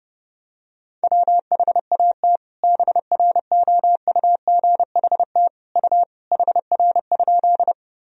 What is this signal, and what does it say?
Morse code, heard as a keyed tone, about 30 wpm, 700 Hz. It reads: WHAT BROUGHT U HR?
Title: Sentences from Top 500 Words and Abbreviations 30wpm